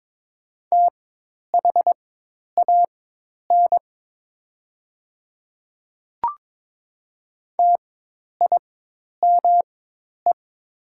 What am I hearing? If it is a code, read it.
Morse code, heard as a keyed tone, about 22 wpm, spaced as for 10 wpm, 700 Hz. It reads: THAN TIME
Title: 4x Character Spacing - Top 100 Words - 22wpm